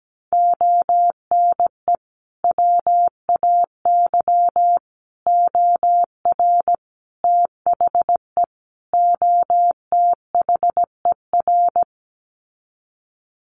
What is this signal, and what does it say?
Morse code, heard as a keyed tone, about 17 wpm, 700 Hz. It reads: ONE WAY OR THE OTHER
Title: Sets of 5 Words 17wpm